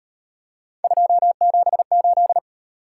Morse code, heard as a keyed tone, about 38 wpm, 700 Hz. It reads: 278